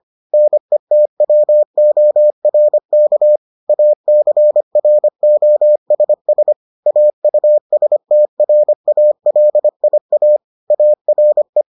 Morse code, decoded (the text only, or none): NETWORK ACROSS AUSTRALIA ARE